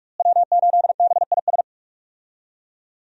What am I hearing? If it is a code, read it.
Morse code, heard as a keyed tone, about 41 wpm, 700 Hz. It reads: W8BIS
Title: USA Call Signs - 5 Characters - Repeated 3x - 45wpm